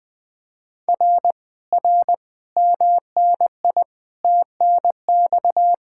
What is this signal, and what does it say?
Morse code, heard as a keyed tone, about 20 wpm, 700 Hz. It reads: R R MNI TNX
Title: Speed-Racing Common QSO Phrases Encore - 20wpm